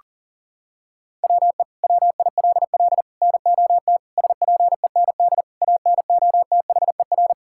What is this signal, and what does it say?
Morse code, heard as a keyed tone, about 40 wpm, 700 Hz. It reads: WE WILL NOT SPEND ANOTHER
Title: Sentences from Top 1000 Words - Review Part 5 - 40wpm